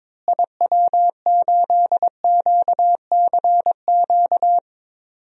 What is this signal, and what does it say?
Morse code, heard as a keyed tone, about 22 wpm, 700 Hz. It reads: IW8QCQ
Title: DXCC-100 Call Signs - 6 Characters - Repeated 3x - 22wpm